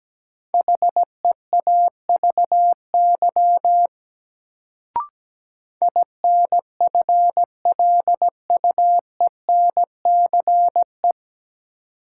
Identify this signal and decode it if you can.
Morse code, heard as a keyed tone, about 17 wpm, 700 Hz. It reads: HEAVY INFLUENCE
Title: Sentences from Top 1100 Words - New Word Drill Alt - 17wpm